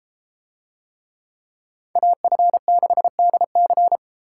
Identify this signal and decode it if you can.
Morse code, heard as a keyed tone, about 33 wpm, 700 Hz. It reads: AF6DC